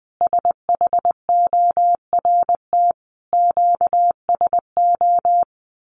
Morse code, heard as a keyed tone, about 20 wpm, 700 Hz. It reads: SHORT QSO
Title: Common QSO Phrases 20wpm